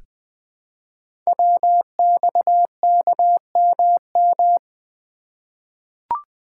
Morse code, heard as a keyed tone, about 20 wpm, 700 Hz. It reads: WXKMM
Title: Code Groups - 5 Letters - 20wpm